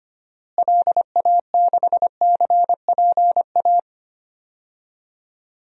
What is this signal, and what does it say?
Morse code, heard as a keyed tone, about 25 wpm, 700 Hz. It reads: LA6CPA
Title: DXCC-100 Call Signs - 6 Characters - Repeated 2x - 25wpm